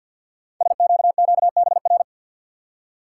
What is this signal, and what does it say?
Morse code, heard as a keyed tone, about 50 wpm, 690 Hz. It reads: SQQBR
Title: Code Groups - 5 Letters - 50wpm